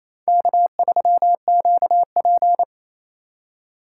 Morse code, heard as a keyed tone, about 37 wpm, 700 Hz. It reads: K3QP